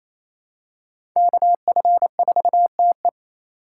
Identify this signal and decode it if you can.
Morse code, heard as a keyed tone, about 28 wpm, 700 Hz. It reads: KF4TE
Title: USA Call Signs - 5 Characters - 4x Character Spacing - 28wpm